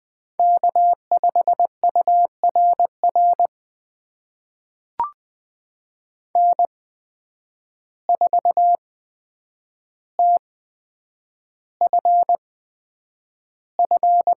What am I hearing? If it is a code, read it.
Morse code, heard as a keyed tone, about 20 wpm, spaced as for 9 wpm, 700 Hz. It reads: K5URR N4TFF
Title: USA Call Signs - 5 Characters - 8x Character Spacing - 20wpm